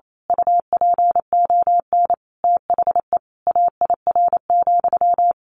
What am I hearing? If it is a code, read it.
Morse code, heard as a keyed tone, about 28 wpm, 700 Hz. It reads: UPON THE AIR,
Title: A Sound Of Thunder 28wpm